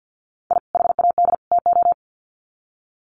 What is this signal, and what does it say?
Morse code, heard as a keyed tone, about 50 wpm, 700 Hz. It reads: I HAD TO
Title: Sets of 3 Words 50wpm